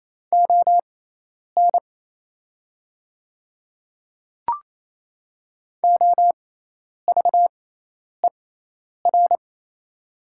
Morse code, heard as a keyed tone, about 28 wpm, spaced as for 10 wpm, 700 Hz. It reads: ON OVER
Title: 6x Character Spacing - Top 100 Words - 28wpm